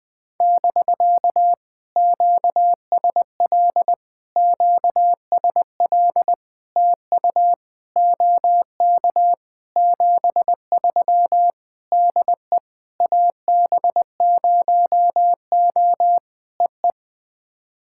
Morse code, heard as a keyed tone, about 20 wpm, 700 Hz. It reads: <BK> QSL QSL TU OK 73 DE AB0O EE
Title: POTA QSO Practice - 20wpm